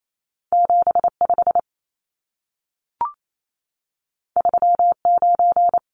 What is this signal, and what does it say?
Morse code, heard as a keyed tone, about 28 wpm, 700 Hz. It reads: 75 39
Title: Numbers - 2-Digit 28wpm